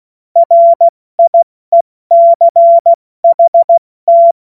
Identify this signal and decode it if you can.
Morse code, heard as a keyed tone, about 16 wpm, spaced as for 14 wpm, 680 Hz. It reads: RIECHT
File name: Morsekurs / Witze / Witz 14. Juni 2022 70 F80